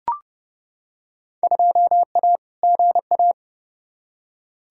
Morse code, heard as a keyed tone, about 30 wpm, 700 Hz. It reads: 2A GA
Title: Field Day Exchanges - 30wpm